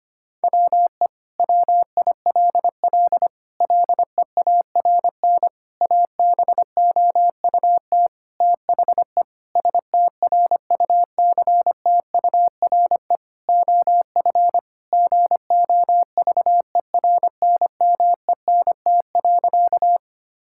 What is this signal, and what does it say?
Morse code, heard as a keyed tone, about 25 wpm, 700 Hz. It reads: WE WILL LEARN ABOUT THE STRUCTURE OF GOVERNMENT.
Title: Sentences from Top 1000 Words - Review Part 6 - 25wpm